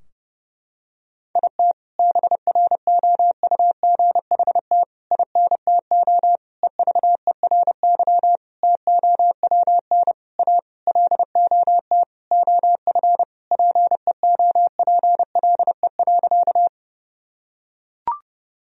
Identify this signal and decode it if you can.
Morse code, heard as a keyed tone, about 32 wpm, 700 Hz. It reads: IT BROUGHT INTO EVERY TOWN A LOT OF PEOPLE.
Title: Sentences from Top 1000 Words - Review Part 3 - 30wpm